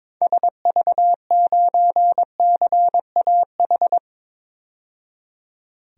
Morse code, heard as a keyed tone, about 22 wpm, 700 Hz. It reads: SV9CAH
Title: DXCC-100 Call Signs - 6 Characters - 22wpm